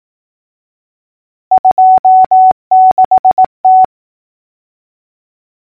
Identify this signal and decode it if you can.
Morse code, heard as a keyed tone, about 18 wpm, 750 Hz. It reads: 26T
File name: numbers / V2-morse-clicks